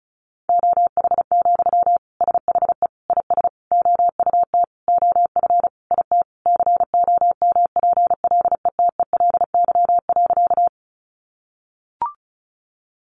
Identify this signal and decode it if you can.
Morse code, heard as a keyed tone, about 34 wpm, 700 Hz. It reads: OH, SHE IS OUT OF IT COMPLETELY.
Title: Sentences from Top 1100 Words 35wpm